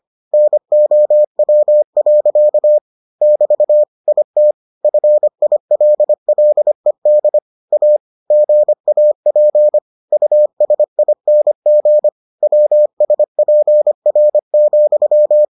Morse code, heard as a keyed tone, about 25 wpm, 600 Hz. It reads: NOW. <BT> IT FILLED A GAP USING WSPR,